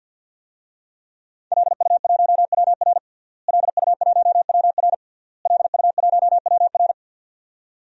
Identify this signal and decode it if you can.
Morse code, heard as a keyed tone, about 48 wpm, 700 Hz. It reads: LU1WR LU1WR LU1WR
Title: DXCC-100 Call Signs - 5 Characters - Repeated 3x - 50wpm